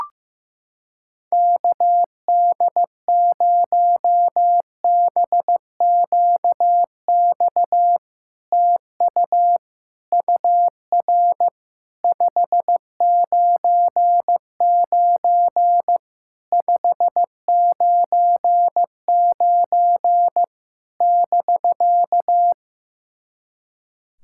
Morse code, heard as a keyed tone, about 15 wpm, 700 Hz. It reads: KD0BQX TU UR 599 599 <BK>
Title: SOTA QSO Practice 15wpm